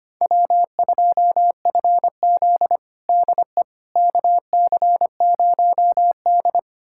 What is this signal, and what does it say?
Morse code, heard as a keyed tone, about 25 wpm, 700 Hz. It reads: W2FZ DE KC0D
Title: USA Callsign DE Callsign 25wpm